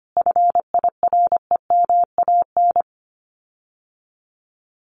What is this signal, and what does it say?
Morse code, heard as a keyed tone, about 25 wpm, 700 Hz. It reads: FIREMAN
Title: Common QSO Phrases Encore 25wpm